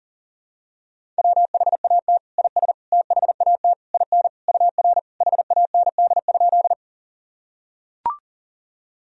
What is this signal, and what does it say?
Morse code, heard as a keyed tone, about 40 wpm, 700 Hz. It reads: WHAT IS THAT IN UR HAND?
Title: Sentences from Top 200 Words and Abbreviations 40wpm